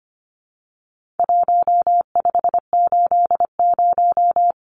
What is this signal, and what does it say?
Morse code, heard as a keyed tone, about 25 wpm, 700 Hz. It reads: 1580
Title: Numbers - Contest Style - 4-Digit 25wpm